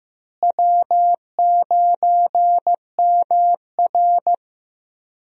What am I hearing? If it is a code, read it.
Morse code, heard as a keyed tone, about 15 wpm, 700 Hz. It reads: W9MR
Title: USA Call Signs - 4 Characters - Repeated 3x - 15wpm